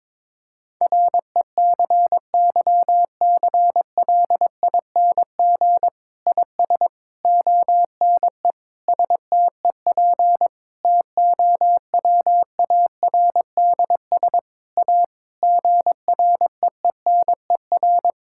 Morse code, heard as a keyed tone, about 22 wpm, 700 Hz. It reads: RECYCLING IS ONE STEP TOWARDS A GREENER
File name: Difficult Sentences with Hints - 22wpm